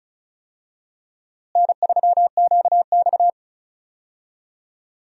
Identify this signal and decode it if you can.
Morse code, heard as a keyed tone, about 36 wpm, 700 Hz. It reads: N3QX
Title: USA Call Signs - 4 Characters - Repeated 3x - 35wpm